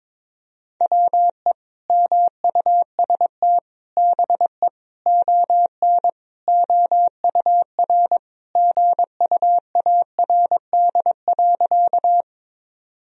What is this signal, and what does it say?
Morse code, heard as a keyed tone, about 21 wpm, 700 Hz. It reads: WE MUST BE ON OUR GUARD.
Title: Sentences from Top 1000 Words - Review Part 4 - 22wpm